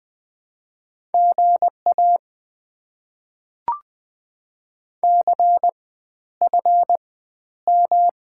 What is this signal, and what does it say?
Morse code, heard as a keyed tone, about 20 wpm, spaced as for 11 wpm, 690 Hz. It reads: GA CFM
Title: Top 90 QSO Elements - 4x Character Spacing - 20wpm